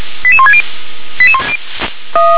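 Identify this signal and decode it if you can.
Five-tone selective call (ZVEI), heard as quick runs of five tones, tones of about 70 ms. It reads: 88160
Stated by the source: ZVEI five-tone call 88220 received by SDR.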